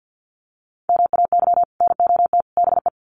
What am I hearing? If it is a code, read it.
Morse code, heard as a keyed tone, about 50 wpm, 700 Hz. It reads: MAY NOT BE
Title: Sets of 3 Words 50wpm